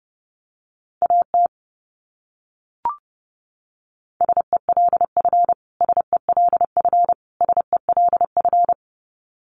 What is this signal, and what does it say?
Morse code, heard as a keyed tone, about 30 wpm, 700 Hz. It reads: AT SELF SELF SELF
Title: Top 200 Words Repeated 30wpm